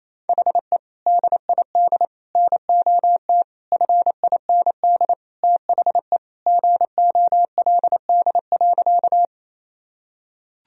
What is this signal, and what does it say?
Morse code, heard as a keyed tone, about 28 wpm, 700 Hz. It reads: HE DID NOT FIND THE GOLD.
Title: Sentences from Top 1000 Words - Review Part 3 - 28wpm